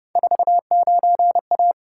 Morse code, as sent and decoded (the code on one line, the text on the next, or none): ....- ----. .-
49A